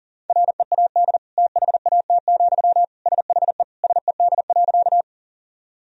Morse code, heard as a keyed tone, about 40 wpm, 700 Hz. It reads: READ THAT, SHE SED.
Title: Sentences from Top 200 Words and Abbreviations 40wpm